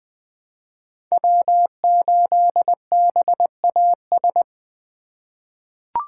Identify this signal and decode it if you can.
Morse code, heard as a keyed tone, about 20 wpm, 700 Hz. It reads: W8BAS
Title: USA Call Signs - 5 Characters - 3x Character Spacing - 20wpm